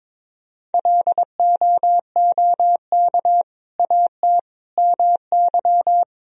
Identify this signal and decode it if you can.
Morse code, heard as a keyed tone, about 22 wpm, 700 Hz. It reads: LOOK AT MY
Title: Sentences from Top 1000 Words - Review Part 3 - 22wpm